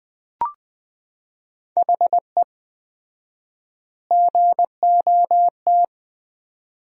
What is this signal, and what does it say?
Morse code, heard as a keyed tone, about 20 wpm, 700 Hz. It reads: HE GOT
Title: 4x Word-Spacing - Sentences from Top 400 Words - 20wpm